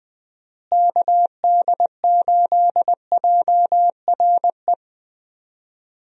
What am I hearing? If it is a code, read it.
Morse code, heard as a keyed tone, about 20 wpm, 700 Hz. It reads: KD8JRE
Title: USA Call Signs - 6 Characters - Repeated 3x - 20wpm